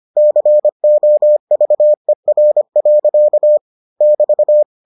Morse code, heard as a keyed tone, about 25 wpm, 600 Hz. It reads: COVER. <BT>